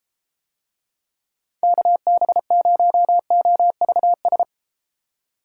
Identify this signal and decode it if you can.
Morse code, heard as a keyed tone, about 33 wpm, 700 Hz. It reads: KB0OVS